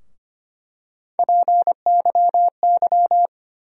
Morse code, heard as a keyed tone, about 25 wpm, 700 Hz. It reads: PYY
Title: USA Call Sign Postfixes - 3 Letters 25wpm